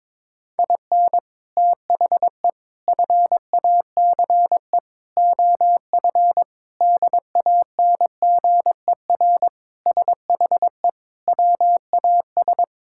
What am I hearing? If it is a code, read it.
Morse code, heard as a keyed tone, about 21 wpm, 700 Hz. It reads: IN THE FACE OF DANGER SHE WAS
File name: practice / 22wpm / Sentences with 4-Word Phrases - 22wpm